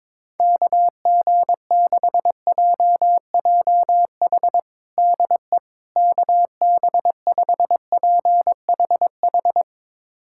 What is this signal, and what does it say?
Morse code, heard as a keyed tone, about 22 wpm, 700 Hz. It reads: KG6JJH DE KB5PHH
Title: USA Callsign DE Callsign 22wpm